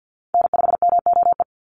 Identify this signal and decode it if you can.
Morse code, heard as a keyed tone, about 50 wpm, 700 Hz. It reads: N5MOE
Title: USA Call Signs 50wpm